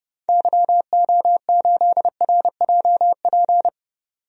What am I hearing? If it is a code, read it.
Morse code, heard as a keyed tone, about 30 wpm, 700 Hz. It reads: YO8RJP
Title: DXCC-100 Call Signs - 6 Characters - 30wpm